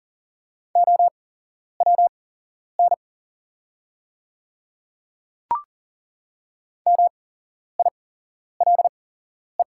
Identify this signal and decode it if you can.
Morse code, heard as a keyed tone, about 40 wpm, spaced as for 11 wpm, 690 Hz. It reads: OWN MILE
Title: 8x Character Spacing - Top 300 Words - 40wpm